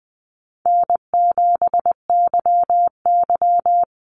Morse code, heard as a keyed tone, about 20 wpm, 700 Hz. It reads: N7YY
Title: Easy Call Signs Repeated 2x - 20wpm